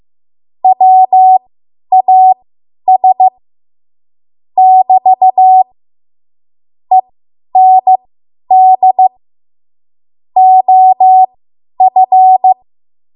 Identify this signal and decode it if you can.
Morse code, heard as a keyed tone, about 15 wpm, spaced as for 10 wpm, 750 Hz. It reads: WAS <BT> END OF
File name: Archive / 10 WPM / 10WPM